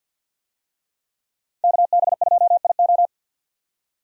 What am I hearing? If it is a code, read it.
Morse code, heard as a keyed tone, about 52 wpm, 700 Hz. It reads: KB2IO
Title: USA Call Signs - 5 Characters - 50wpm